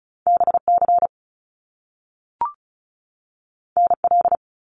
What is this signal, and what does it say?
Morse code, heard as a keyed tone, about 36 wpm, 700 Hz. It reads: BC NL
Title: Canadian Provinces and Territories 35wpm